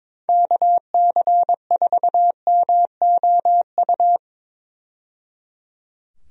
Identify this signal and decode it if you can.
Morse code, heard as a keyed tone, about 22 wpm, 700 Hz. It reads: KC4MOU
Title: USA Call Signs - 6 Characters - 22wpm